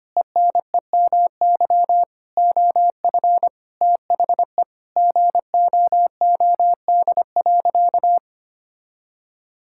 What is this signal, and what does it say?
Morse code, heard as a keyed tone, about 25 wpm, 700 Hz. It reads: ENEMY OF THE GOOD.
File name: practice / 25wpm / Sentences with 9-Word Phrases - 25wpm